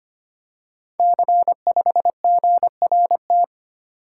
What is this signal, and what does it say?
Morse code, heard as a keyed tone, about 25 wpm, 700 Hz. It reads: C5GRT
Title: Code Groups - 5 Letter:Numbers - 25wpm